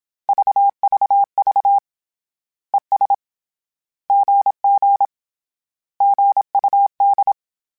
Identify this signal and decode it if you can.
Morse code, heard as a keyed tone, about 27 wpm, 800 Hz. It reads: VVV ES GG GUD
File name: Misc / Misc-mp3s / Words @25